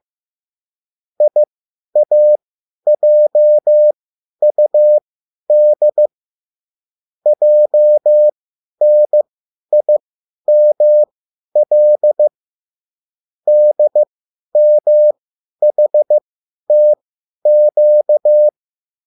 Morse code, heard as a keyed tone, about 15 wpm, spaced as for 10 wpm, 600 Hz. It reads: IAJUD JNIML DMHTQ